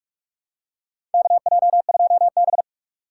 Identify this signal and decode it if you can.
Morse code, heard as a keyed tone, about 44 wpm, 700 Hz. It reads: KJ2B
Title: USA Call Signs - 3x Character Spacing - 45wpm